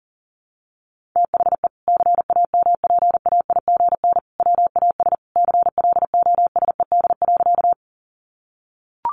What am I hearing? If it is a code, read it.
Morse code, heard as a keyed tone, about 40 wpm, 700 Hz. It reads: THE CAMPAIGN WAS CLOSED.
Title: Sentences from Top 1400 Words 40wpm